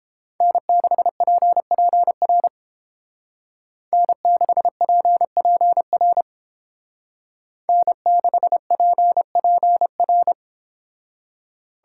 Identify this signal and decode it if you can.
Morse code, heard as a keyed tone, about 30 wpm, 700 Hz. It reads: N6PPR N6PPR N6PPR